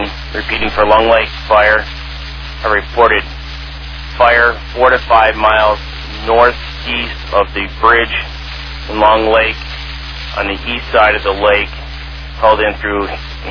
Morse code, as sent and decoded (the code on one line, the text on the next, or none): none